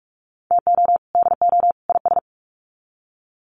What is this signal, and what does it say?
Morse code, heard as a keyed tone, about 44 wpm, 700 Hz. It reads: TO DO IS
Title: Sets of 3 Words 45wpm